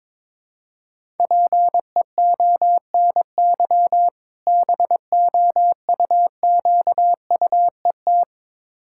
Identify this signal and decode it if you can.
Morse code, heard as a keyed tone, about 22 wpm, 690 Hz. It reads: PEONY BOUQUET
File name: Difficult Words with Hints - 22wpm